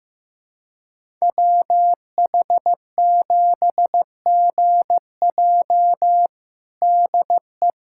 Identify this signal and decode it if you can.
Morse code, heard as a keyed tone, about 15 wpm, 700 Hz. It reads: WH7GJ DE
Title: USA Callsign DE Callsign 15wpm